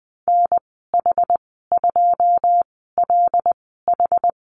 Morse code, heard as a keyed tone, about 20 wpm, spaced as for 14 wpm, 700 Hz. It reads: NH2LH
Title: 122 Lesson 20g - Number 3 - Call Signs 20wpm